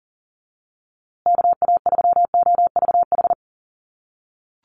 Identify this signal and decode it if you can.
Morse code, heard as a keyed tone, about 40 wpm, 700 Hz. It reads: KA3OVH
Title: USA Call Signs Repeated 2x 40wpm